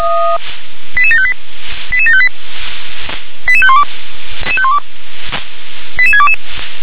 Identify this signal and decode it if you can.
Five-tone selective call (ZVEI), heard as quick runs of five tones, tones of about 70 ms. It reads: 88758 88758 88512 88622